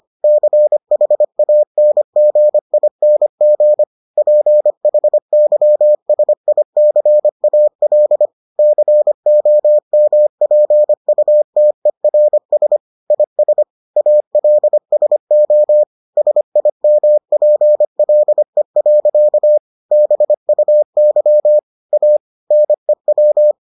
Morse code, heard as a keyed tone, about 25 wpm, 600 Hz. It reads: CHANGING PHYSICAL COMPUTERS IS ALSO SIMPLE. BUY A NEW